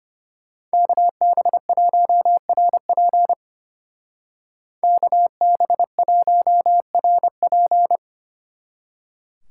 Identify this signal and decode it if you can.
Morse code, heard as a keyed tone, about 33 wpm, 700 Hz. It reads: KB1RP KB1RP